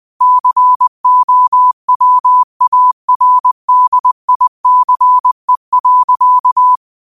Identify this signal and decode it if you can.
Morse code, heard as a keyed tone, about 20 wpm, 1000 Hz. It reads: COWARDICE.
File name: 20wpm